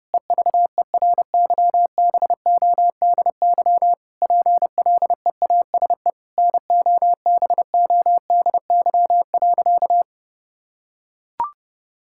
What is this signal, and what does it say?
Morse code, heard as a keyed tone, about 30 wpm, 700 Hz. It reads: EVERYBODY PLEASE NOBODY.
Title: Sentences from Top 1600 Words 30wpm